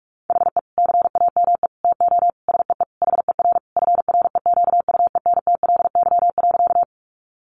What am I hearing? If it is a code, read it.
Morse code, heard as a keyed tone, about 44 wpm, 700 Hz. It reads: HE CAME TO SEE HER FREQUENTLY.
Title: Sentences from Top 1300 Words 45wpm